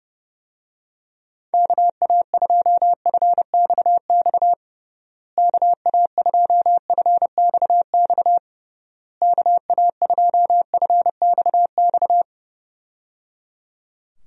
Morse code, heard as a keyed tone, about 29 wpm, 700 Hz. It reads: KA2FXX KA2FXX KA2FXX
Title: USA Call Signs - 6 Characters - Repeated 3x - 30wpm